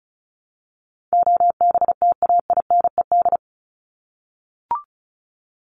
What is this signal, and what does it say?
Morse code, heard as a keyed tone, about 34 wpm, 700 Hz. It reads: OBTAINED
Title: ED Suffix - Top 100 Words with Base Words 35wpm